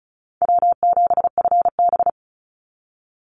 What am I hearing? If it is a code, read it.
Morse code, heard as a keyed tone, about 34 wpm, 700 Hz. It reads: W7FB
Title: Easy Call Signs Repeated 3x - 35wpm